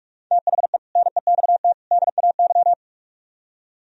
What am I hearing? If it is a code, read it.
Morse code, heard as a keyed tone, about 46 wpm, 700 Hz. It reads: THE NEXT DAY